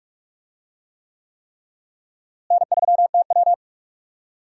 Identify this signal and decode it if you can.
Morse code, heard as a keyed tone, about 41 wpm, 700 Hz. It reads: N3TW
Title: USA Call Signs - 4 Characters - Repeated 3x - 45wpm